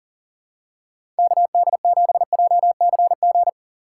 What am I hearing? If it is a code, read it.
Morse code, heard as a keyed tone, about 40 wpm, 700 Hz. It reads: KD7JCG
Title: USA Call Signs - 6 Characters - 3x Character Spacing - 40wpm